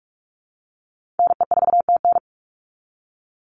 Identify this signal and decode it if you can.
Morse code, heard as a keyed tone, about 44 wpm, 700 Hz. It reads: NE4TN
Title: NA SOTA Chaser Callsigns 45wpm